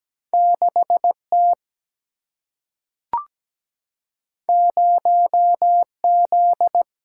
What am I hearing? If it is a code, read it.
Morse code, heard as a keyed tone, about 17 wpm, 700 Hz. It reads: 6T 0Z
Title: Single-Letter Single-Number 17wpm